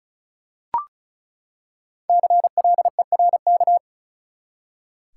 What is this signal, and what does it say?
Morse code, heard as a keyed tone, about 34 wpm, 700 Hz. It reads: CLERK
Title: Five Letter Words - 35wpm